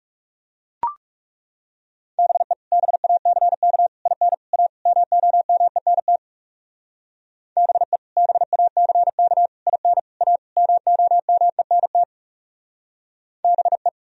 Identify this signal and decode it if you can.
Morse code, heard as a keyed tone, about 37 wpm, 700 Hz. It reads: BE BACK IN A MOMENT BE BACK IN A MOMENT BE
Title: Speed-Racing - Sets of 5 Words 30wpm